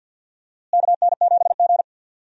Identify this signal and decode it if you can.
Morse code, heard as a keyed tone, about 49 wpm, 700 Hz. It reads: KN7G
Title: USA Call Signs - 4 Characters - Repeated 2x - 50wpm